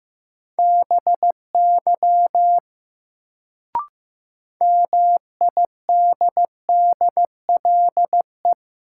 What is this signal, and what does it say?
Morse code, heard as a keyed tone, about 15 wpm, 700 Hz. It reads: BY MIDDLE
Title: Sets of 2 Words - New Word Drill - 15wpm